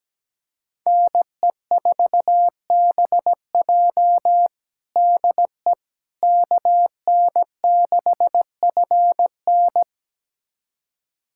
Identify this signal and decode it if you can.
Morse code, heard as a keyed tone, about 17 wpm, 700 Hz. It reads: NE4BJ DE KN6FN
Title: USA Callsign DE Callsign - 5 Characters - 17wpm